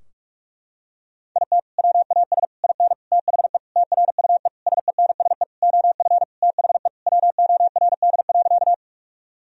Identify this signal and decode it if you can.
Morse code, heard as a keyed tone, about 41 wpm, 700 Hz. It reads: IT WAS IN THE TRUE SENSE OF THE WORD.
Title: Sentences from Top 1000 Words - Review Part 4 - 45wpm